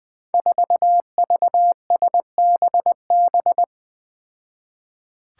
Morse code, heard as a keyed tone, about 20 wpm, 700 Hz. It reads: 4VSBB
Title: Five Challenging Letter-Number Combinations 20wpm